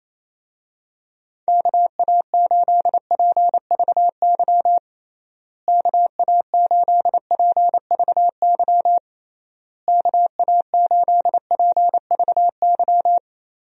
Morse code, heard as a keyed tone, about 28 wpm, 700 Hz. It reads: KA8PVY KA8PVY KA8PVY
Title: USA Call Signs - 6 Characters - Repeated 3x - 28wpm